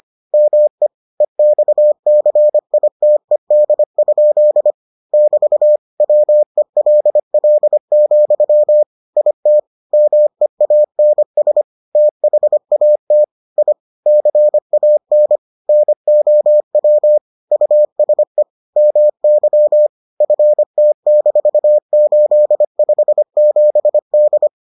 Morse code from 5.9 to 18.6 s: WELL, IT MEANS THAT I CAN NOW USE

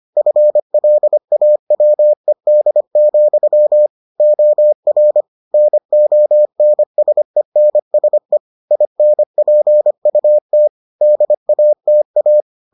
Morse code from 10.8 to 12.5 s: DATA

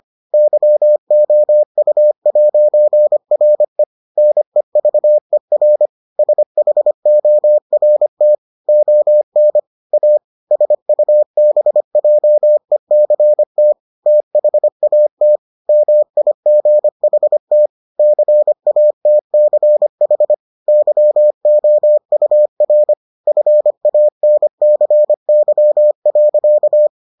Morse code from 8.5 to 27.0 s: ON A SUBJECT THAT MIGHT CATCH YOUR FANCY.

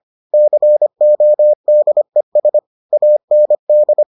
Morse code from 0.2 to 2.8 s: CODES

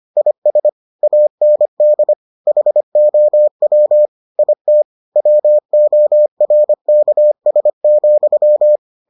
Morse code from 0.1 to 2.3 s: IS AND

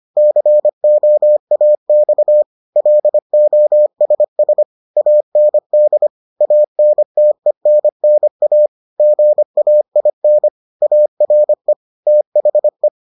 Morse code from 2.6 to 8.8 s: LOSS AND ANTENNA

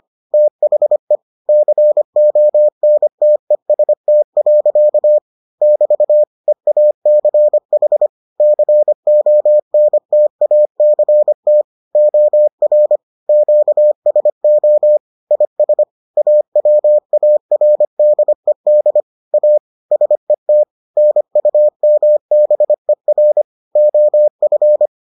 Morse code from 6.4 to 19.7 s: EACH CONTACT OR QSO IS AWARDED A

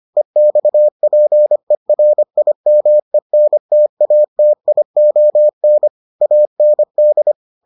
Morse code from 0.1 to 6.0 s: EXPERIMENTATION

